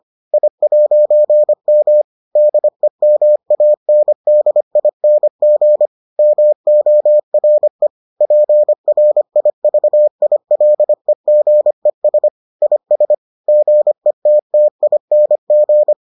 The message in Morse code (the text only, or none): I'M DEMANDING MORE PRIVILEGES IS GETTING